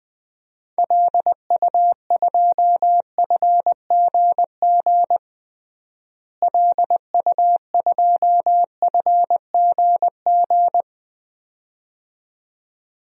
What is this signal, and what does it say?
Morse code, heard as a keyed tone, about 20 wpm, 700 Hz. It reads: LU2FGG LU2FGG
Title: DXCC-100 Call Signs - 6 Characters - Repeated 2x - 20wpm